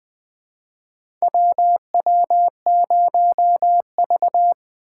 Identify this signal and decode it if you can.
Morse code, heard as a keyed tone, about 20 wpm, 700 Hz. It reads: WW0V
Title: USA Call Signs - 4 Characters - 2x Character Spacing - 20wpm